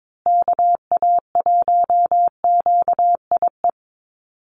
Morse code, heard as a keyed tone, about 22 wpm, 700 Hz. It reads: KA1QIE
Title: USA Call Signs Repeated 3x 22wpm